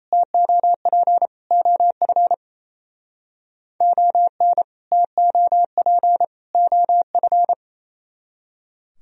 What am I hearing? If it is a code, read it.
Morse code, heard as a keyed tone, about 37 wpm, 700 Hz. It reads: TOP OF ON TOP OF